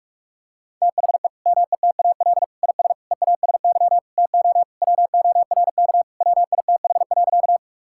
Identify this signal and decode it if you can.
Morse code, heard as a keyed tone, about 46 wpm, 700 Hz. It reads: THE METAL IS EASY TO WORK WITH.
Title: Sentences from Top 1000 Words - Review Part 6 - 45wpm